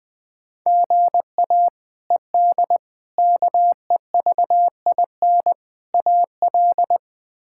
Morse code, heard as a keyed tone, about 20 wpm, 700 Hz. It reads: GA ED KEVIN AL
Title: K1USN SST QSO Practice - 20wpm